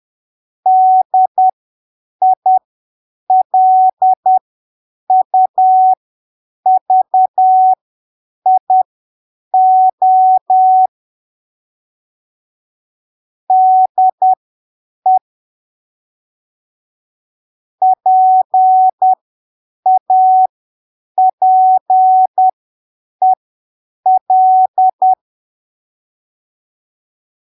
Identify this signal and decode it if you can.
Morse code, heard as a keyed tone, about 10 wpm, spaced as for 7 wpm, 750 Hz. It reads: DILUVIO DE PAPEL